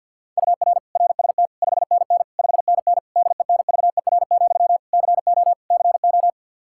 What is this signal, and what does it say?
Morse code, heard as a keyed tone, about 48 wpm, 700 Hz. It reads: UR RST 5NN 5NN DENVER, CO CO
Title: Practice QSO - Exchange 1 - Beginning - 50wpm